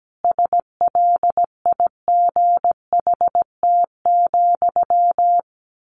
Morse code, heard as a keyed tone, about 17 wpm, 700 Hz. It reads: SLIGHT,
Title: A Sound Of Thunder 17wpm